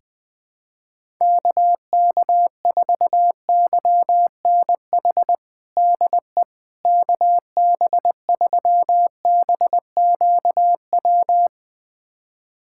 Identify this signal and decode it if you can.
Morse code, heard as a keyed tone, about 20 wpm, 700 Hz. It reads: KK4YNH DE KB3BQW
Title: USA Callsign DE Callsign - 6 Characters - 20wpm